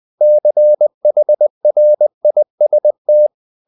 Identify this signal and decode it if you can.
Morse code, heard as a keyed tone, about 20 wpm, 600 Hz. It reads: CHRIST
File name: the old man and the sea / the old man and the sea part 1 20wpm